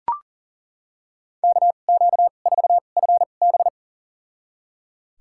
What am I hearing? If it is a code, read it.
Morse code, heard as a keyed tone, about 38 wpm, spaced as for 28 wpm, 700 Hz. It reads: KQ4FB
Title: USA Call Signs - 5 Characters - 2x Character Spacing - 40wpm